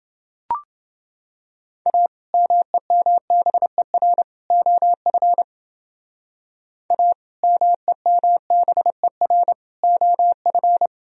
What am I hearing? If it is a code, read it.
Morse code, heard as a keyed tone, about 29 wpm, 700 Hz. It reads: A MEMBER OF A MEMBER OF